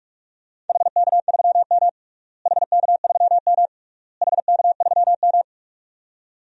Morse code, heard as a keyed tone, about 48 wpm, 690 Hz. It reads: HK3M HK3M HK3M
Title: DXCC-100 Call Signs - 4 Characters - Repeated 3x - 45wpm